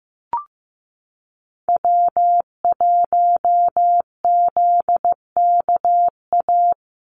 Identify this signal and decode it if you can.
Morse code, heard as a keyed tone, about 15 wpm, 700 Hz. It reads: W1ZKA
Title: USA Call Signs Repeated 3x 15wpm